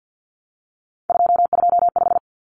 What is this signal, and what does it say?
Morse code, heard as a keyed tone, about 50 wpm, 700 Hz. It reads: 225
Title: Common Cut Numbers - 3-Digit 50wpm